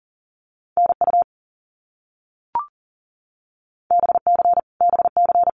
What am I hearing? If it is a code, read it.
Morse code, heard as a keyed tone, about 40 wpm, 700 Hz. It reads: NU BC BC
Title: Canadian Provinces and Territories - Repeated 3x - 40wpm